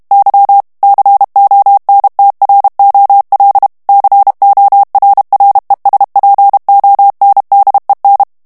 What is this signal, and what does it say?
Morse code, heard as a keyed tone, about 32 wpm, 800 Hz. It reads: Y CONTROL CORRESPONDEN
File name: morse mp3 / ESPAGNOL TEXTE EN CLAIR 2 VITESSE 30 MOTS MINUTES